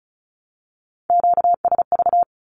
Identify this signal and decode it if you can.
Morse code, heard as a keyed tone, about 34 wpm, 700 Hz. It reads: QSV
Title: Q-Codes Repeated 3x 35wpm